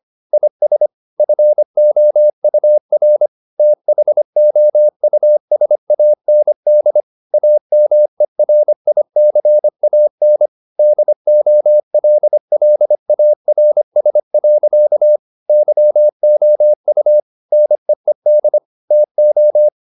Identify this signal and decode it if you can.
Morse code, heard as a keyed tone, about 25 wpm, 600 Hz. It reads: IS FOUR THOUSAND AMERICAN DOLLARS. YOU NEED TO